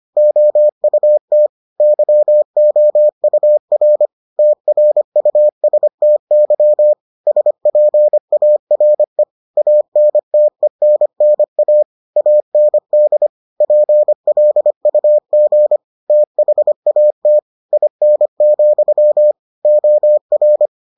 Morse code, heard as a keyed tone, about 25 wpm, 600 Hz. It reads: OUT YOUR TRUSTY SPARE ANTENNA AND PLUG THAT IN, OR